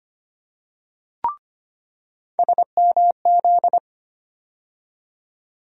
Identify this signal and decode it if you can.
Morse code, heard as a keyed tone, about 25 wpm, 700 Hz. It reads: SMZ
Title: USA Call Sign Postfixes - 3 Letters 25wpm